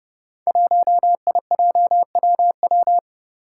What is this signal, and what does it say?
Morse code, heard as a keyed tone, about 30 wpm, 700 Hz. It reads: 1IJWW
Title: Five Challenging Letter-Number Combinations 30wpm